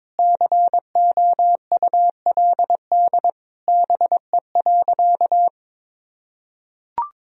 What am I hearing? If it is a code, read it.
Morse code, heard as a keyed tone, about 22 wpm, 700 Hz. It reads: COULD BE.
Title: Sentences from Top 1000 Words - Review Part 3 - 22wpm